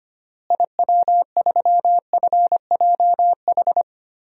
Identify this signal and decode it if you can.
Morse code, heard as a keyed tone, about 25 wpm, 700 Hz. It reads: IW3FJH
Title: DXCC-100 Call Signs - 6 Characters - Repeated 3x - 25wpm